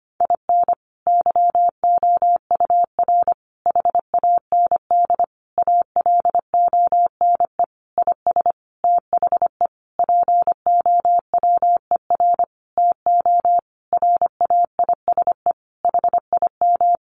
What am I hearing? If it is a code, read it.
Morse code, heard as a keyed tone, about 25 wpm, 700 Hz. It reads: IN YOUR HAND ALONE IS THE POWER TO RAISE HIM
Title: Sentences from Top 1400 Words 25wpm